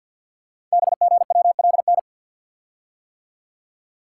Morse code, heard as a keyed tone, about 53 wpm, 700 Hz. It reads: BGWLN
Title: Code Groups - 5 Letters - 50wpm